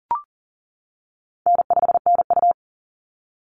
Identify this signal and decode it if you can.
Morse code, heard as a keyed tone, about 40 wpm, 700 Hz. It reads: N5NU
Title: Call Signs - 40wpm